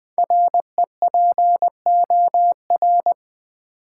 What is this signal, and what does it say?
Morse code, heard as a keyed tone, about 20 wpm, 700 Hz. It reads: REPOR
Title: Build A Word - 5 Letters At A Time - 10-Letter Words - 20wpm